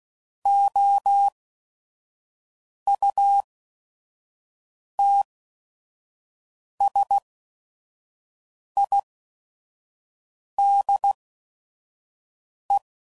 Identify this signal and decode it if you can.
Morse code, heard as a keyed tone, about 16 wpm, spaced as for 5 wpm, 800 Hz. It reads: OUTSIDE
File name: CW Code Groups @5WPM / Weather Sets / Weather 4 17